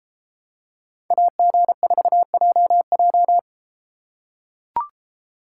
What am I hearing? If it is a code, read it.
Morse code, heard as a keyed tone, about 31 wpm, 700 Hz. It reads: AG4JJ